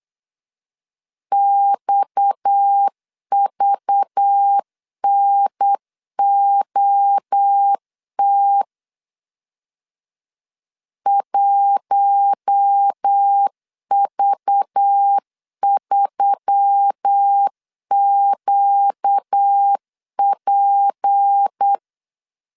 Morse code, heard as a keyed tone, about 8 wpm, 800 Hz. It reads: XVNOT 1V3QP
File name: CW / DET / 1 6 1500C S B 8wpm 5142